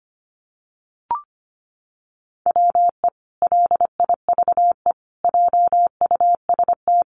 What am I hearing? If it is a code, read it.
Morse code, heard as a keyed tone, about 25 wpm, 700 Hz. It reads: WE LIVE JUST
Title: Sentences from Top 1300 Words 25wpm